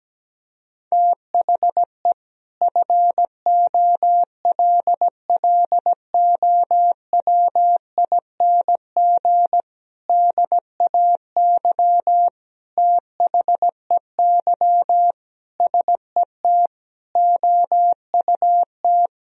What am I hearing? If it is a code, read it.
Morse code, heard as a keyed tone, about 17 wpm, 700 Hz. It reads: THE FOLLOWING DAY THEY SET OUT
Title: Sentences from Top 1000 Words - Review Part 4 - 17wpm